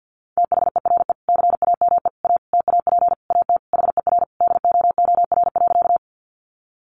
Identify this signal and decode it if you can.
Morse code, heard as a keyed tone, about 50 wpm, 700 Hz. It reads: THERE CAME A TAP AT HER DOOR.
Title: Sentences from Top 1400 Words 50wpm